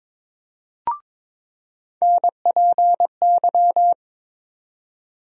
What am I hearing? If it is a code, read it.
Morse code, heard as a keyed tone, about 22 wpm, 700 Hz. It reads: NPY
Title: USA Call Sign Postfixes - 3 Letters 22wpm